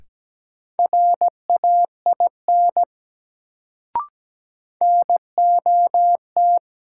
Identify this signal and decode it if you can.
Morse code, heard as a keyed tone, about 17 wpm, 700 Hz. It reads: RAIN NOT
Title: Sentences from Top 600 Words - New Word Drill - 17wpm